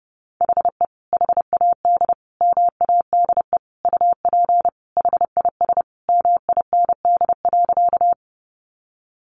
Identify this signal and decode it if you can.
Morse code, heard as a keyed tone, about 30 wpm, 700 Hz. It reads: HE HAD MADE UP HIS MIND.
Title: Sentences from Top 500 Words 30wpm